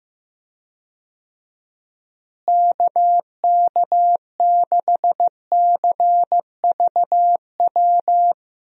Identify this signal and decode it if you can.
Morse code, heard as a keyed tone, about 15 wpm, 700 Hz. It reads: KK6CVW
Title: USA Call Signs - 6 Characters - Repeated 3x - 15wpm